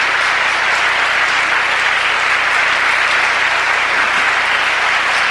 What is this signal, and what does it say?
Emergency Alert System (SAME) tones, heard as harsh buzzing bursts. It is heard from 0.9 to 4.5 s.